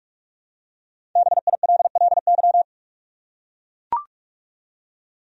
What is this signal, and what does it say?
Morse code, heard as a keyed tone, about 42 wpm, 700 Hz. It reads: BILLY
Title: USA Common Names - Max 5 Letters - 45wpm